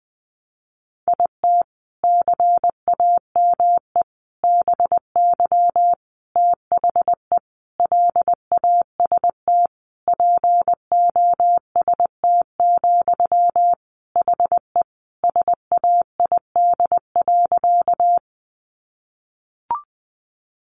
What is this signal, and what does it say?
Morse code, heard as a keyed tone, about 20 wpm, 700 Hz. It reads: IT CAME BY THE LAST POST, HE SAID.
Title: Sentences from Top 1000 Words 20wpm